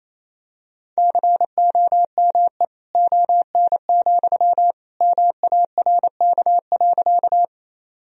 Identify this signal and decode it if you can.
Morse code, heard as a keyed tone, about 28 wpm, 700 Hz. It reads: COME ON, MARK.
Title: Sentences from Top 1000 Words - Review Part 1 - 28wpm